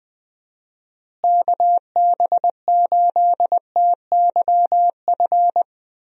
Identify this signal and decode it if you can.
Morse code, heard as a keyed tone, about 20 wpm, 700 Hz. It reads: KB8TYF
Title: USA Call Signs - 6 Characters - Repeated 2x - 20wpm